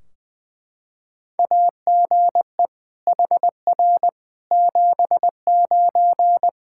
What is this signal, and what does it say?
Morse code, heard as a keyed tone, about 20 wpm, 700 Hz. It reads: AGE HR 79
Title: Practice QSO - Exchange 3 - More Info - 20wpm